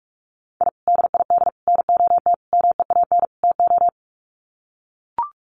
Morse code, heard as a keyed tone, about 44 wpm, 700 Hz. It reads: I DID NOT MEAN TO
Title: Sets of 5 Words 45wpm